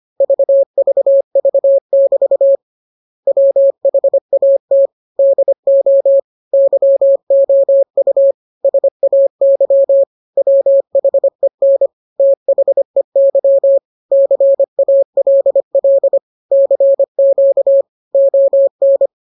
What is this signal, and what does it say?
Morse code, heard as a keyed tone, about 25 wpm, 550 Hz. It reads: VVV<BT> WHAT DO YOU SAY WHEN THEY CALL CQ ON